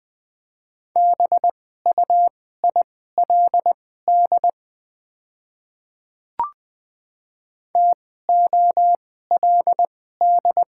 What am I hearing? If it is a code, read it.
Morse code, heard as a keyed tone, about 20 wpm, spaced as for 14 wpm, 690 Hz. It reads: BUILD TOLD
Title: 2x Character Spacing - Top 500 Words - 20wpm